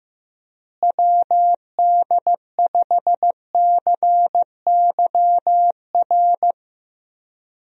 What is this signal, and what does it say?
Morse code, heard as a keyed tone, about 15 wpm, 700 Hz. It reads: WD5CYR
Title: USA Call Signs - 6 Characters - Repeated 3x - 15wpm